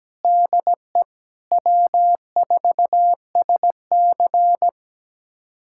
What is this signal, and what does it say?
Morse code, heard as a keyed tone, about 17 wpm, 700 Hz. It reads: DE W4SC
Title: USA Callsign DE Callsign - 4 Characters - 17wpm